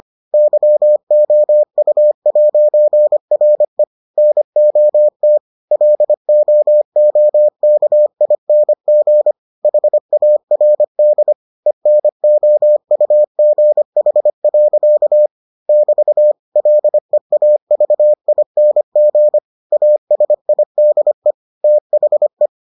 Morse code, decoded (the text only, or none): YOU'RE NOT LOOKING HARD ENOUGH. <BT> LEAVING ASIDE THE